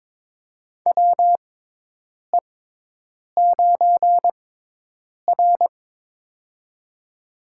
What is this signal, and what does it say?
Morse code, heard as a keyed tone, about 22 wpm, spaced as for 8 wpm, 700 Hz. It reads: WE9R
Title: USA Call Signs - 4 Characters - 6x Character Spacing - 22wpm